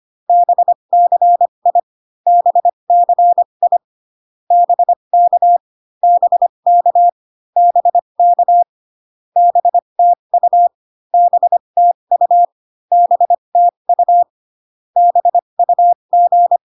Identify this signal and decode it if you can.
Morse code, heard as a keyed tone, about 25 wpm, spaced as for 22 wpm, 700 Hz. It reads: BCI BCI BK BK BK BTU BTU BTU BUG
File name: CW common abbreviations alphabetical repeated 3 times 25 18wpm